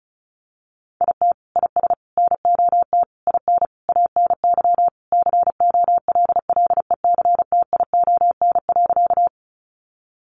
Morse code, heard as a keyed tone, about 34 wpm, 700 Hz. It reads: IT IS NOT IN ANY COLLECTION.
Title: Sentences from Top 1200 Words 35wpm